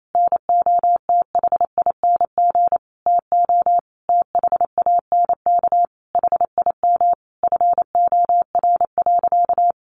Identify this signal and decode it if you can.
Morse code, heard as a keyed tone, about 28 wpm, 700 Hz. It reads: NOTHING TO THANK HIM FOR.
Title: Sentences from Top 1000 Words 28wpm